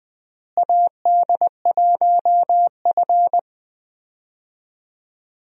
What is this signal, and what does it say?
Morse code, heard as a keyed tone, about 21 wpm, 700 Hz. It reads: AD1F
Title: USA Call Signs - 4 Characters - Repeated 3x - 20wpm